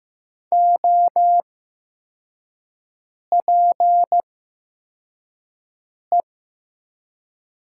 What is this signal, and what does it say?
Morse code, heard as a keyed tone, about 15 wpm, spaced as for 4 wpm, 700 Hz. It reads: OPE
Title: 8x Character Spacing - Top 300 Words - 15wpm